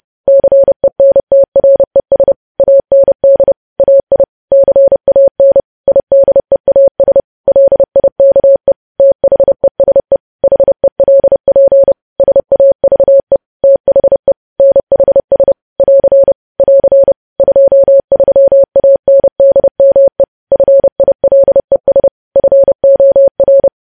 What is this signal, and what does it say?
Morse code, heard as a keyed tone, about 30 wpm, 550 Hz. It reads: CENTRES AND AI CAN IDEAS LIKE THESE HELP SAVE THE NHS <AR> <AR> 23ANDME FILES FOR
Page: news headlines in morse code at 30 wpm